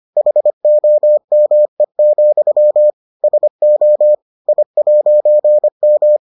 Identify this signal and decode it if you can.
Morse code, heard as a keyed tone, about 25 wpm, 600 Hz. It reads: HOME, SO I'M